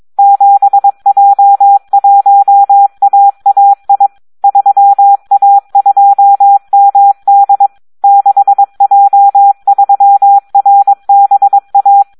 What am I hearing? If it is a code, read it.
Morse code, heard as a keyed tone, about 22 wpm, 800 Hz. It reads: 7J1AAI 3A2MD 6J3RBA
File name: Audio practica CW / SD 1 22